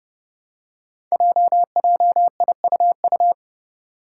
Morse code, heard as a keyed tone, about 30 wpm, 700 Hz. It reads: JJIUU
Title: Five Challenging Letter-Number Combinations 30wpm